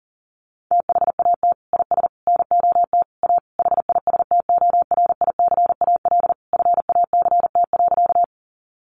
Morse code, heard as a keyed tone, about 40 wpm, 700 Hz. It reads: THAT IS NOT A HISTORICAL FACT.
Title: Sentences from Top 600 Words 40wpm